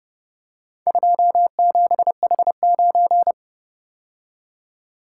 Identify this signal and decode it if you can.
Morse code, heard as a keyed tone, about 30 wpm, 700 Hz. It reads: 27H9